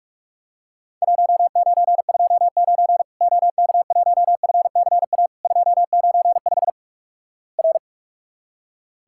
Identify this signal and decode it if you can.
Morse code, heard as a keyed tone, about 46 wpm, 700 Hz. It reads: 1929 OK1FCA 295 R
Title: Worked All Europe DX Contest - QTC Exchange - 45wpm